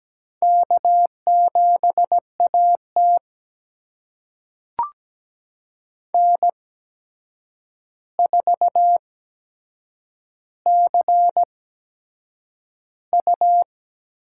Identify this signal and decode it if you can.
Morse code, heard as a keyed tone, about 17 wpm, spaced as for 7 wpm, 700 Hz. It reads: K7AT N4CU
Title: USA Call Signs - 4 Characters - 8x Character Spacing - 17wpm